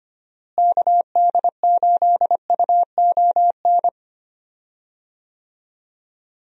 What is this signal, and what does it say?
Morse code, heard as a keyed tone, about 25 wpm, 700 Hz. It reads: KD8UON
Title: USA Call Signs - 6 Characters - Repeated 2x - 25wpm